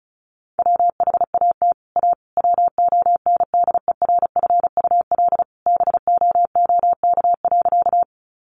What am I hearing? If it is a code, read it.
Morse code, heard as a keyed tone, about 34 wpm, 700 Hz. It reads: WHAT A WONDERFUL BOOK.
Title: Sentences from Top 1100 Words 35wpm